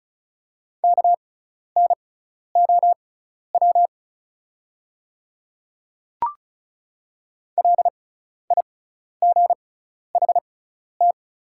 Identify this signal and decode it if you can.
Morse code, heard as a keyed tone, about 36 wpm, spaced as for 12 wpm, 690 Hz. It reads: KNOW LIGHT
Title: 6x Character Spacing - Top 200 Words - 35wpm